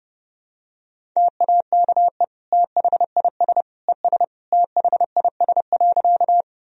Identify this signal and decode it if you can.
Morse code, heard as a keyed tone, about 32 wpm, 700 Hz. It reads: TAKE THIS ES THIS.
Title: Sentences from Top 200 Words and Abbreviations 30wpm